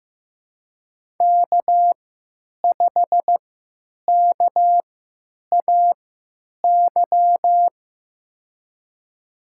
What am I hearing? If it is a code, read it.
Morse code, heard as a keyed tone, about 15 wpm, spaced as for 8 wpm, 700 Hz. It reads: K5KAY
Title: USA Call Signs - 5 Characters - 3x Character Spacing - 15wpm